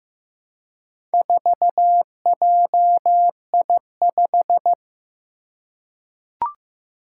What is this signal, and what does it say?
Morse code, heard as a keyed tone, about 15 wpm, 700 Hz. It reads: 4JI5
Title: Four Challenging Letter-Number Combinations 15wpm